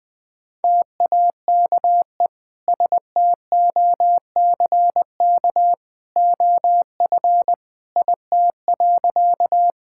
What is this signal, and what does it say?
Morse code, heard as a keyed tone, about 20 wpm, 700 Hz. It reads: TAKE STOCK OF IT.
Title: Sentences from Top 1000 Words - Review Part 5 - 20wpm